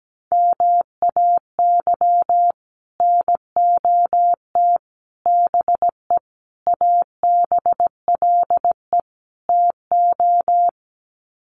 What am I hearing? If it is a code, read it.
Morse code, heard as a keyed tone, about 17 wpm, 700 Hz. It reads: MAY NOT BE ABLE TO
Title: Sets of 5 Words 17wpm